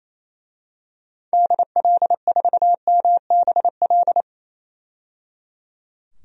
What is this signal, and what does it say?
Morse code, heard as a keyed tone, about 28 wpm, 700 Hz. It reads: DL4MBL
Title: DXCC-100 Call Signs - 6 Characters - 28wpm